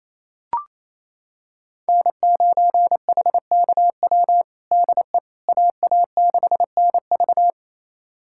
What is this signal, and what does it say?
Morse code, heard as a keyed tone, about 28 wpm, 700 Hz. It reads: N9HKW DE AA6NV
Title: USA Callsign DE Callsign - 5 Characters - 28wpm